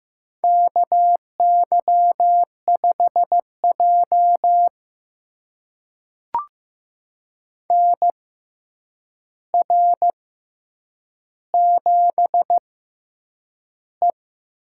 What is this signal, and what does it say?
Morse code, heard as a keyed tone, about 15 wpm, spaced as for 8 wpm, 700 Hz. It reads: KY5J NR7E
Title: USA Call Signs - 4 Characters - 6x Character Spacing - 15wpm